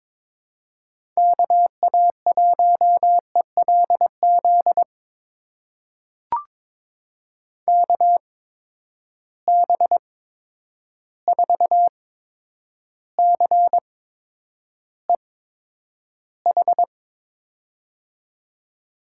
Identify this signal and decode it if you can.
Morse code, heard as a keyed tone, about 22 wpm, spaced as for 9 wpm, 700 Hz. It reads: KA1ELZ KB4CEH
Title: USA Call Signs - 8x Character Spacing - 22wpm